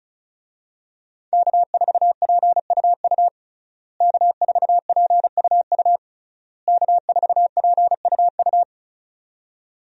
Morse code, heard as a keyed tone, about 36 wpm, 700 Hz. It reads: K4PUU K4PUU K4PUU
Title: USA Call Signs - 5 Characters - Repeated 3x - 35wpm